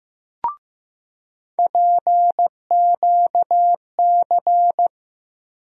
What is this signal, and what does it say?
Morse code, heard as a keyed tone, about 15 wpm, 700 Hz. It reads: PQC
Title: USA Call Sign Postfixes - 3 Letters 15wpm